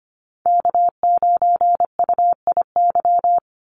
Morse code, heard as a keyed tone, about 25 wpm, 700 Hz. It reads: K9UIY
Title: Call Signs Repeated 2x - 25wpm